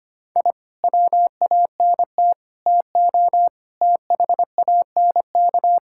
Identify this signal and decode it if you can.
Morse code, heard as a keyed tone, about 25 wpm, 700 Hz. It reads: I WANT TO THANK
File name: practice / 25wpm / Sentences with 9-Word Phrases - 25wpm